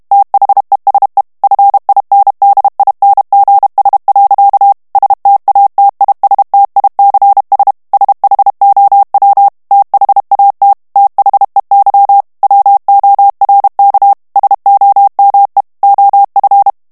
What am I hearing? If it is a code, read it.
Morse code, heard as a keyed tone, about 32 wpm, 800 Hz. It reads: THESE FINDINGS. STATISTICS SHOW THAT THEY WORK SOME OF
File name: morse mp3 / ANGLAIS texte clair 1 vitesse 30 mots minute